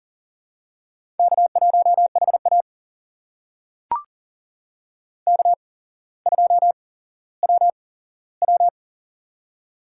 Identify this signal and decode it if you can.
Morse code, heard as a keyed tone, about 40 wpm, spaced as for 17 wpm, 700 Hz. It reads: K1HA K2WW
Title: USA Call Signs - 4 Characters - 8x Character Spacing - 40wpm